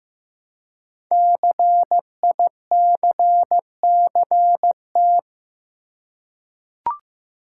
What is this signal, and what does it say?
Morse code, heard as a keyed tone, about 15 wpm, 700 Hz. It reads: CICCT
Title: Code Groups - 5 Letters - 15wpm